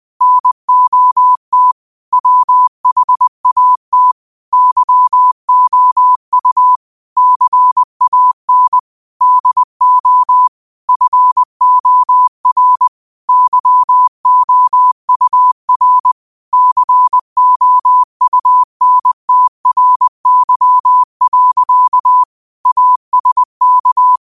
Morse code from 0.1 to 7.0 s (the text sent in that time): NOT WHAT YOU